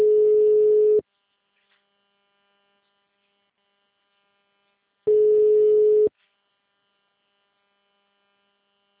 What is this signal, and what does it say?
Ringback tone (Europe, 425 Hz).
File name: out- -2026- - - .27882